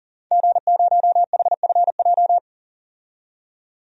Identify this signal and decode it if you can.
Morse code, heard as a keyed tone, about 40 wpm, 700 Hz. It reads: G0HFJ
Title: DXCC-100 Call Signs - 5 Characters - 40wpm